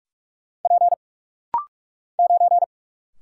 Morse code, heard as a keyed tone, about 44 wpm, 700 Hz. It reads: P 9
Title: Single Letter-Number Alt - Rapid-Fire 45wpm